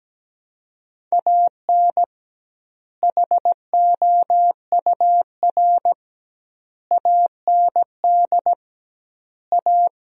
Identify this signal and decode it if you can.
Morse code, heard as a keyed tone, about 17 wpm, 700 Hz. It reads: AN HOUR AND A
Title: 2x Word-Spacing - Sentences from Top 400 Words - 17wpm